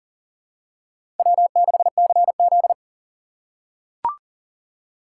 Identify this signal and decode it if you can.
Morse code, heard as a keyed tone, about 39 wpm, 700 Hz. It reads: W6CZ
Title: USA Call Signs - 6x Character Spacing - 40wpm